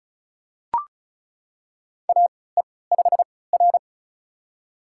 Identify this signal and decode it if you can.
Morse code, heard as a keyed tone, about 36 wpm, spaced as for 20 wpm, 700 Hz. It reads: AE5R
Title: USA Call Signs - 4 Characters - 3x Character Spacing - 35wpm